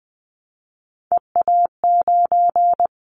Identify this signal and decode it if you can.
Morse code, heard as a keyed tone, about 20 wpm, 700 Hz. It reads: EA9
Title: Callsign Prefixes for Top 100 DXCC - 20wpm